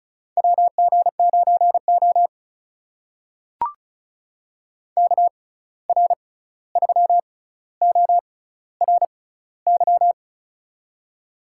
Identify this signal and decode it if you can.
Morse code, heard as a keyed tone, about 36 wpm, spaced as for 16 wpm, 700 Hz. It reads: WG9O KR3ORY
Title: USA Call Signs - 6x Character Spacing - 35wpm